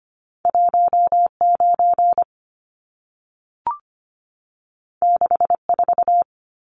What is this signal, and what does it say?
Morse code, heard as a keyed tone, about 25 wpm, 700 Hz. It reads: 19 64
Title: Numbers - 2-Digit 25wpm